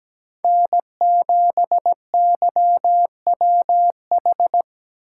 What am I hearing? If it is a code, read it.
Morse code, heard as a keyed tone, about 17 wpm, 700 Hz. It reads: N7YWH
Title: Speed Racing - USA 1x3 Call Signs 17wpm